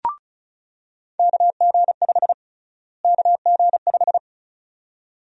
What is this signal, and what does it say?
Morse code, heard as a keyed tone, about 34 wpm, 700 Hz. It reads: KG5 KG5
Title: USA Call Sign Prefixes - Repeated 2x - 35wpm